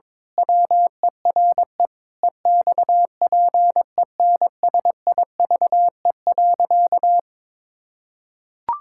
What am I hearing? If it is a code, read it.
Morse code, heard as a keyed tone, about 22 wpm, 700 Hz. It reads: WERE EXPENSIVE.
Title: Pangram Challenge v2 - 22wpm